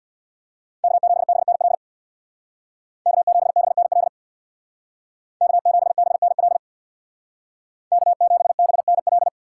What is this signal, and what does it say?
Morse code, heard as a keyed tone, about 64 wpm, 700 Hz. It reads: K7BNL K7BNL K7BNL K7BNL